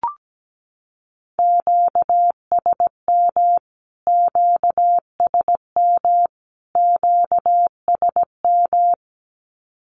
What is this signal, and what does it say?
Morse code, heard as a keyed tone, about 17 wpm, 700 Hz. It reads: QSM QSM QSM
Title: Q-Codes Repeated 3x 17wpm